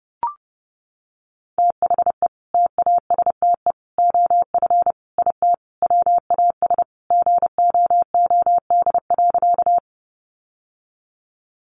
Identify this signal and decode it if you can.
Morse code, heard as a keyed tone, about 30 wpm, 700 Hz. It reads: THE TASTE OF IT WAS GOOD.
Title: Sentences from Top 1100 Words 30wpm